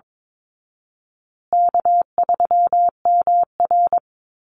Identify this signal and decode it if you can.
Morse code, heard as a keyed tone, about 22 wpm, 700 Hz. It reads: K3MR
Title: Easy Call Signs Repeated 3x - 22wpm